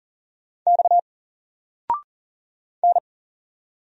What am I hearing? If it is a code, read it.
Morse code, heard as a keyed tone, about 40 wpm, 700 Hz. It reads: X N
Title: Single Letter-Number 40wpm